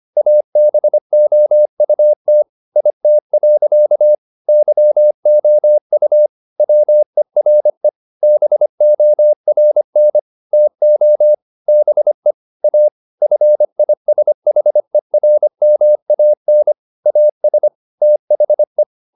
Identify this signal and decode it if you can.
Morse code, heard as a keyed tone, about 25 wpm, 600 Hz. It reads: ABOUT IT. YOU WERE BORN TO BE A FISHERMAN AS THE